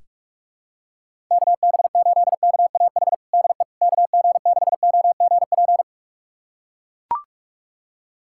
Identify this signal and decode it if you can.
Morse code, heard as a keyed tone, about 42 wpm, 700 Hz. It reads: KB8KAH DE KG6OGP
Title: USA Callsign DE Callsign - 6 Characters - 45wpm